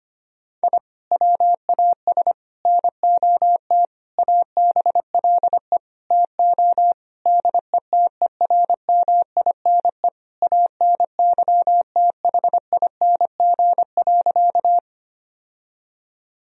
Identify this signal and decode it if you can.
Morse code, heard as a keyed tone, about 25 wpm, 700 Hz. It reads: I WAS NOT ABLE TO DETERMINE ANYTHING.
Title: Sentences from Top 1000 Words - Review Part 4 - 25wpm